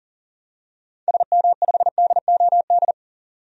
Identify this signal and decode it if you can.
Morse code, heard as a keyed tone, about 40 wpm, 700 Hz. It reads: SM5DOD
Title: DXCC-100 Call Signs - 6 Characters - 40wpm